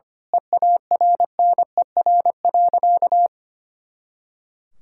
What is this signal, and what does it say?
Morse code, heard as a keyed tone, about 25 wpm, 700 Hz. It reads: EARNER.
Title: Pangram Challenge v2 - 25wpm